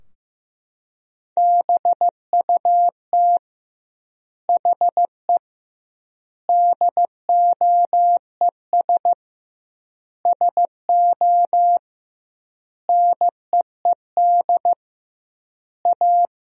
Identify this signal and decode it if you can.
Morse code, heard as a keyed tone, about 15 wpm, 700 Hz. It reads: BUT HE DOES SO NEED A
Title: 2x Word-Spacing - Sentences from Top 200 Words - 15wpm